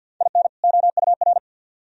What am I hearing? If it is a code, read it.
Morse code, heard as a keyed tone, about 62 wpm, 700 Hz. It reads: IN OUR
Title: Speed-Racing - Sets of 2 Words 50wpm